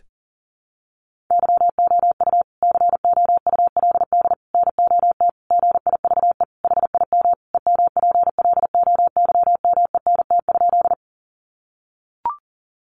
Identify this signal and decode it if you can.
Morse code, heard as a keyed tone, about 40 wpm, 700 Hz. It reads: YOU COULD NOT GIVE HIM EMPLOYMENT?
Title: Sentences from Top 1500 Words 40wpm